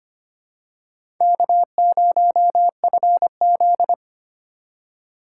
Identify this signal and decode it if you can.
Morse code, heard as a keyed tone, about 25 wpm, 700 Hz. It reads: K0FZ
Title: USA Call Signs - 4 Characters - 2x Character Spacing - 25wpm